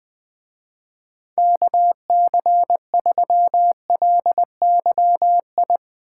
Morse code, heard as a keyed tone, about 22 wpm, 700 Hz. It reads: KC3LYI
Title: Speed Racing - USA 2x3 Call Signs 20wpm